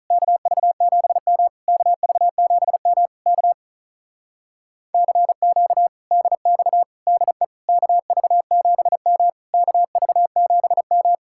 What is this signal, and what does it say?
Morse code, heard as a keyed tone, about 43 wpm, 700 Hz. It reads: KV7M KV7M K CQ DX DE KV7M KV7M
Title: Speed-Racing Common QSO Phrases Encore - 35wpm